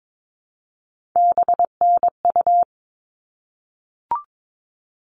Morse code, heard as a keyed tone, about 22 wpm, 700 Hz. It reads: BNU
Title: All Cut Numbers - 3-Digit 22wpm